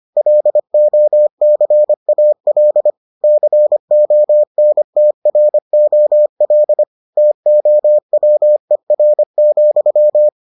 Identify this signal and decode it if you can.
Morse code, heard as a keyed tone, about 25 wpm, 600 Hz. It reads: LOCAL CONTROL TOWER,